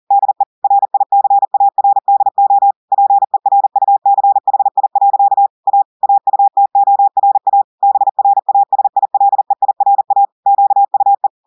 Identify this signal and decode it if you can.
Morse code, heard as a keyed tone, about 40 wpm, 800 Hz. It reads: DE RICARDO PERUCHI. A AUTORA BRASILEIRA QUE
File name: audios / portugais / portugais texte en clair 1 vitesse 40 mots minutes